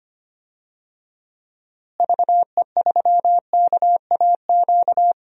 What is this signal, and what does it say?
Morse code, heard as a keyed tone, about 25 wpm, 700 Hz. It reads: VE3KAQ
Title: DXCC-100 Call Signs - 6 Characters - Repeated 2x - 25wpm